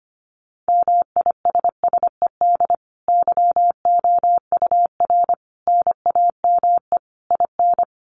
Morse code, heard as a keyed tone, about 25 wpm, 700 Hz. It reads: MISSED YOUR NAME IN
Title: Common QSO Phrases 25wpm